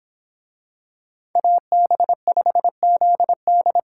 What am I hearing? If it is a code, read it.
Morse code, heard as a keyed tone, about 25 wpm, 700 Hz. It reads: AB5ZD